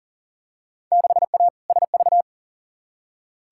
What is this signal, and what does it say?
Morse code, heard as a keyed tone, about 40 wpm, 700 Hz. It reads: 6A SV
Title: Field Day Exchanges Encore - 40wpm